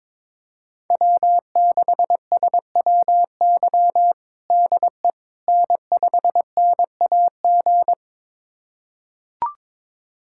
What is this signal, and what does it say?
Morse code, heard as a keyed tone, about 22 wpm, 700 Hz. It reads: W6SWY DE N5NAG
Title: USA Callsign DE Callsign 22wpm